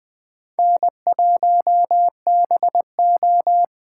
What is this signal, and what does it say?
Morse code, heard as a keyed tone, about 20 wpm, 700 Hz. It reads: N1BO
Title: Speed Racing - USA 1x2 Call Signs 17wpm